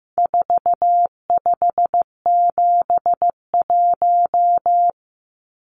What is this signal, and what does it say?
Morse code, heard as a keyed tone, about 15 wpm, 700 Hz. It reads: 4571
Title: Numbers - 4-Digit 15wpm